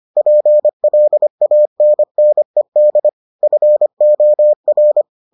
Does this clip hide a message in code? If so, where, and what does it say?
Yes: Morse code from 0.2 to 5.0 s, heard as a keyed tone, about 25 wpm, 600 Hz. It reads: PLANNED FOR